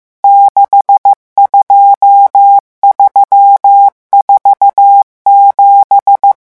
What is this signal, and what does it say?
Morse code, heard as a keyed tone, about 15 wpm, 800 Hz. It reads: 62347
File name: morse mp3 / revision11 chiffres vitesse 14